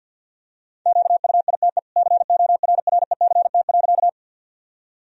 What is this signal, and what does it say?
Morse code, heard as a keyed tone, about 46 wpm, 700 Hz. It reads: QUITE CORRECT.
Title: Sentences from Top 1000 Words - Review Part 2 - 50wpm